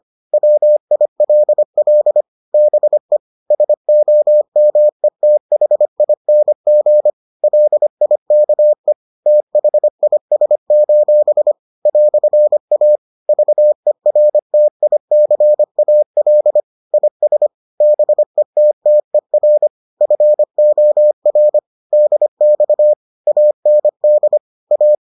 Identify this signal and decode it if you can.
Morse code, heard as a keyed tone, about 25 wpm, 600 Hz. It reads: WILL BE SOMETHING LIKE THIS: "A VERTICAL IS BETTER FOR DX AND A